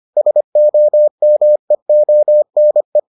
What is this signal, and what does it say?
Morse code, heard as a keyed tone, about 25 wpm, 600 Hz. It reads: SOMEONE